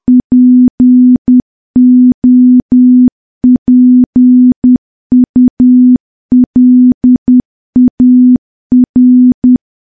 Morse code, heard as a keyed tone, about 10 wpm, 250 Hz. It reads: POPULAR
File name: morse / morse-training-10WPM